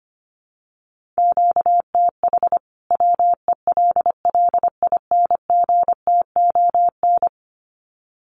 Morse code, heard as a keyed tone, about 25 wpm, 700 Hz. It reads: QTH WELLINGTON
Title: Common QSO Phrases 25wpm